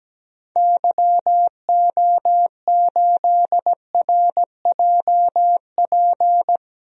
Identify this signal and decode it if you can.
Morse code, heard as a keyed tone, about 17 wpm, 700 Hz. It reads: YO8RJP
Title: DXCC-100 Call Signs - 6 Characters - 17wpm